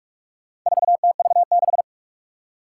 Morse code, heard as a keyed tone, about 44 wpm, 700 Hz. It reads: 4TV6
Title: Four Challenging Letter-Number Combinations 45wpm